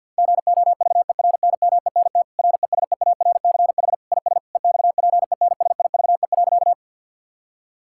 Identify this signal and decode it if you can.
Morse code, heard as a keyed tone, about 50 wpm, 700 Hz. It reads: GOVERNMENT RESEARCH IS EXPENSIVE.
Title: Sentences from Top 1000 Words - Review Part 6 - 50wpm